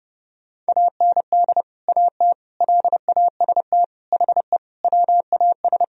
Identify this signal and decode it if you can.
Morse code, heard as a keyed tone, about 30 wpm, 700 Hz. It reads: AND AT LAST HE WAS
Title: Sentences from Top 1000 Words - Review Part 2 - 30wpm